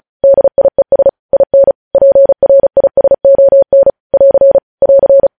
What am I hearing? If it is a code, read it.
Morse code, heard as a keyed tone, about 36 wpm, 550 Hz. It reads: DIES IN PRISON <AR> <AR>